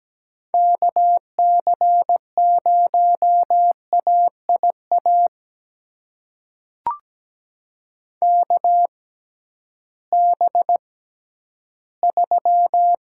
Morse code, heard as a keyed tone, about 17 wpm, spaced as for 11 wpm, 700 Hz. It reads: KC0AIA KB3
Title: USA Call Signs - 6 Characters - 6x Character Spacing - 17wpm